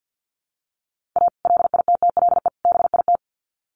Morse code, heard as a keyed tone, about 50 wpm, 700 Hz. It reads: A LITTLE BIT
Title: Sets of 3 Words 50wpm